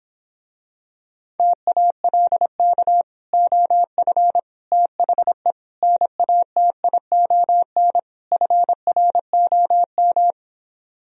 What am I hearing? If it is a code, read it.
Morse code, heard as a keyed tone, about 26 wpm, 700 Hz. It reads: TALK OF THE NATION FROM